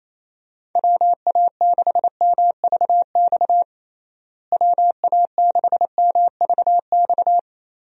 Morse code, heard as a keyed tone, about 27 wpm, 700 Hz. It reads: WA6MVX WA6MVX
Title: USA Call Signs - 6 Characters - Repeated 2x - 28wpm